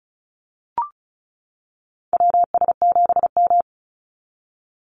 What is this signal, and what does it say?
Morse code, heard as a keyed tone, about 34 wpm, 700 Hz. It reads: WS7M
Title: Call Signs - 35wpm